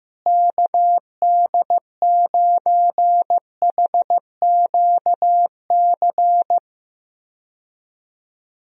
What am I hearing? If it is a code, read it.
Morse code, heard as a keyed tone, about 15 wpm, 700 Hz. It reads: KD9HQC
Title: USA Call Signs - 6 Characters - Repeated 3x - 15wpm